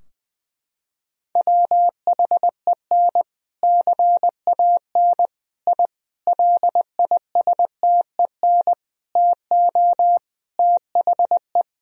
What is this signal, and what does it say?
Morse code, heard as a keyed tone, about 21 wpm, 700 Hz. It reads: WHEN CAN I LISTEN TO THE
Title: Sentences from Top 1000 Words - Review Part 3 - 20wpm